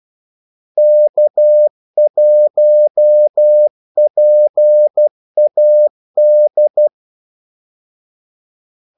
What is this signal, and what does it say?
Morse code, heard as a keyed tone, about 12 wpm, 600 Hz. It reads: K1PAD
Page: 2 ekstra ordmellemrum mellem kaldesignalerne